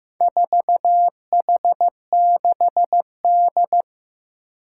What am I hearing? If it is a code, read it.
Morse code, heard as a keyed tone, about 15 wpm, 700 Hz. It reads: 4H6D
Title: Four Challenging Letter-Number Combinations 15wpm